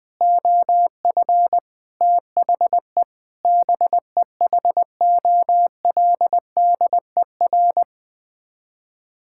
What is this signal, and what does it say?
Morse code, heard as a keyed tone, about 20 wpm, 700 Hz. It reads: OF THE BEHOLDER
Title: Sets of 8 Words - 20wpm